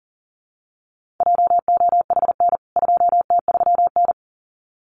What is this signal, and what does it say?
Morse code, heard as a keyed tone, about 40 wpm, 700 Hz. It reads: JOHN 2T3N
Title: CWT Exchange 40wpm